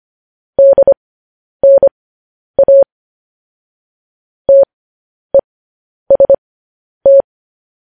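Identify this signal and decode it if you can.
Morse code, heard as a keyed tone, about 25 wpm, spaced as for 10 wpm, 550 Hz. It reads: DNA TEST
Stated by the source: Morse code transmission of news headlines.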